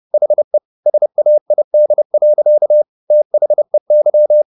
Morse code, heard as a keyed tone, about 30 wpm, 600 Hz. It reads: HE SAID. THEY
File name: the old man and the sea / the old man and the sea 1 30wpm